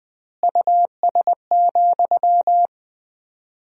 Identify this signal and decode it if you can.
Morse code, heard as a keyed tone, about 20 wpm, 700 Hz. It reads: US,
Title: 4x Word-Spacing - Sentences from Top 400 Words - 20wpm